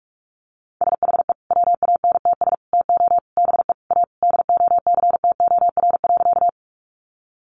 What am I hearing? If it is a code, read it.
Morse code, heard as a keyed tone, about 44 wpm, 700 Hz. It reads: SHE WANTS TO BE A DOCTOR.
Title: Easy Phrases 45wpm